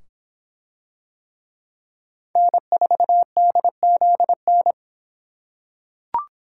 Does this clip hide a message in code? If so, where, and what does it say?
Yes: Morse code from 2.4 to 4.7 s, heard as a keyed tone, about 25 wpm, 700 Hz. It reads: N4DZN